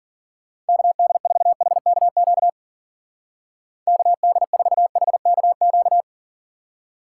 Morse code, heard as a keyed tone, about 50 wpm, 700 Hz. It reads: KD4HKQ KD4HKQ
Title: Speed Racing - USA 2x3 Call Signs 40wpm